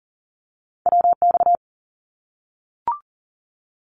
Morse code, heard as a keyed tone, about 40 wpm, 700 Hz. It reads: WX
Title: Abbreviations Repeated 3x - 40wpm